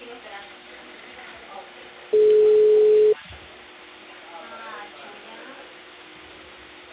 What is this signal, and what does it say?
Ringback tone (Europe, 425 Hz), with speech.